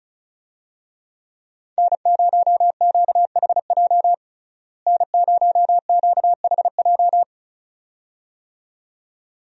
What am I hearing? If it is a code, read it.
Morse code, heard as a keyed tone, about 36 wpm, 700 Hz. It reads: N0QHJ N0QHJ
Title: USA Call Signs - 5 Characters - Repeated 2x - 35wpm